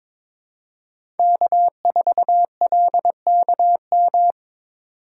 Morse code, heard as a keyed tone, about 26 wpm, 700 Hz. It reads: K4LKM
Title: Speed Racing - USA 1x3 Call Signs 22wpm